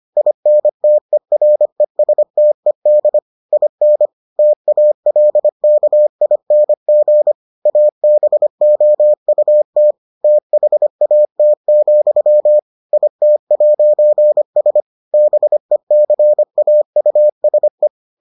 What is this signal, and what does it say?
Morse code, heard as a keyed tone, about 25 wpm, 600 Hz. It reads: INTERESTED IN TALKING ABOUT THAT, IT'S BECAUSE